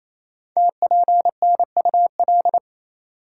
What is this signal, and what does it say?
Morse code, heard as a keyed tone, about 29 wpm, 700 Hz. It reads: TPNUL
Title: Code Groups - 5 Letters - 28wpm